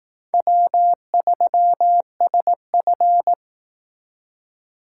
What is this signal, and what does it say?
Morse code, heard as a keyed tone, about 18 wpm, 700 Hz. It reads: W3SF
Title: Speed Racing - USA 1x2 Call Signs 15wpm